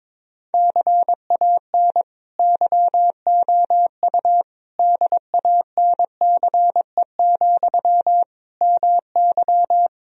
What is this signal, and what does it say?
Morse code, heard as a keyed tone, about 22 wpm, 700 Hz. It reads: CAN YOU DANCE, MY
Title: Sentences from Top 1000 Words - Review Part 4 - 22wpm